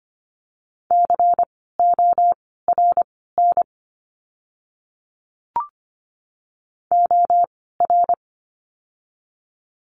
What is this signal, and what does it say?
Morse code, heard as a keyed tone, about 24 wpm, spaced as for 16 wpm, 700 Hz. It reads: CORN OR
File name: course / 25:10wpm / 094 Lesson 16f - Letter c - Words 25wpm